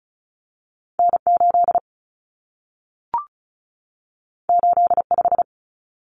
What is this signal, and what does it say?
Morse code, heard as a keyed tone, about 36 wpm, 700 Hz. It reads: N8 85
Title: Common Cut Numbers - 2-Digit 35wpm